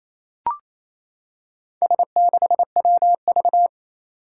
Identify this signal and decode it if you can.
Morse code, heard as a keyed tone, about 27 wpm, 700 Hz. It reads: S6WV
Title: Four Challenging Letter-Number Combinations 28wpm